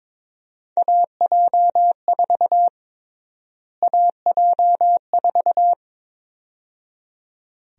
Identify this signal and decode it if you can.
Morse code, heard as a keyed tone, about 22 wpm, 700 Hz. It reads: AJ4 AJ4
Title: USA Call Sign Prefixes - Repeated 2x - 22wpm